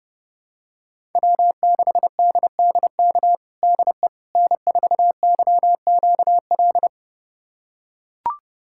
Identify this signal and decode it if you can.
Morse code, heard as a keyed tone, about 32 wpm, 700 Hz. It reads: W6DDK DE N4YQL
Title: USA Callsign DE Callsign - 5 Characters - 30wpm